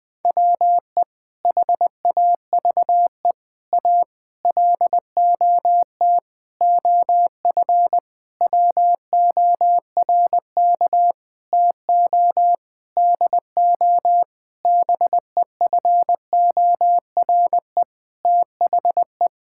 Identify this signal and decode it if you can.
Morse code, heard as a keyed tone, about 20 wpm, 700 Hz. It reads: WE HAVE A LOT OF WORK TO DO BEFORE THE
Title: Sentences with 8-Word Phrases - 20wpm